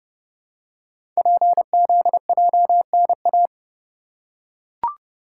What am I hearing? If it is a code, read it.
Morse code, heard as a keyed tone, about 29 wpm, 700 Hz. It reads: PZJNA
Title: Code Groups - 5 Letters - 30wpm